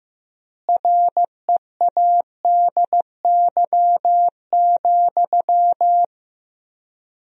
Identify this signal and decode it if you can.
Morse code, heard as a keyed tone, about 15 wpm, 700 Hz. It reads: READY,
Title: 3x Word-Spacing - Sentences from Top 500 Words - 15wpm